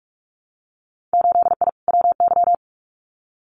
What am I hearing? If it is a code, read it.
Morse code, heard as a keyed tone, about 46 wpm, 700 Hz. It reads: 8I WY
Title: Winter Field Day Exchanges Encore 45wpm